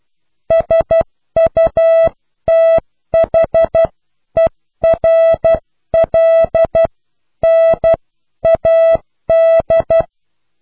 Morse code, heard as a keyed tone, about 12 wpm, spaced as for 10 wpm, 660 Hz. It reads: SUTHERLNAD
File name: cw / pseudo worden(10)